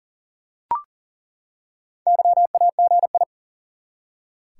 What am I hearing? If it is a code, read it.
Morse code, heard as a keyed tone, about 40 wpm, 700 Hz. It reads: YAGI
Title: Top 180 QSO Elements 40wpm